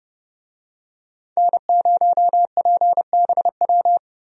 Morse code, heard as a keyed tone, about 30 wpm, 700 Hz. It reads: N0PBW
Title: USA Call Signs - 5 Characters - Repeated 2x - 30wpm